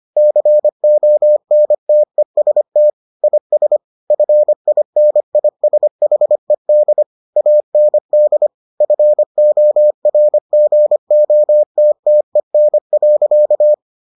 Morse code, heard as a keyed tone, about 25 wpm, 600 Hz. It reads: CONTEST IS FINISHED AND FORGOTTEN.